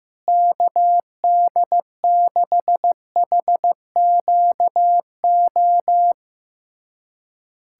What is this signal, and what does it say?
Morse code, heard as a keyed tone, about 15 wpm, 700 Hz. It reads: KD6HQO
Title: USA Callsign DE Callsign 15wpm